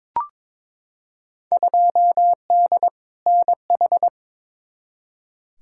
Morse code, heard as a keyed tone, about 22 wpm, 700 Hz. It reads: 2D NH
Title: Field Day Exchanges Encore - 22wpm